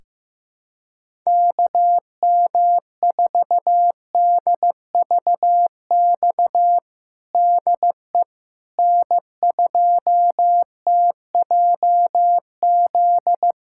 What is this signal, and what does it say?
Morse code, heard as a keyed tone, about 15 wpm, 700 Hz. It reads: KM4DVX DE N2TJZ
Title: USA Callsign DE Callsign 15wpm